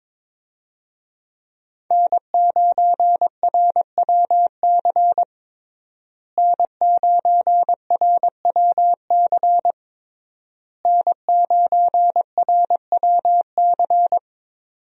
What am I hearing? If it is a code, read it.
Morse code, heard as a keyed tone, about 22 wpm, 700 Hz. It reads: N9RWC N9RWC N9RWC
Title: USA Call Signs - 5 Characters - Repeated 3x - 22wpm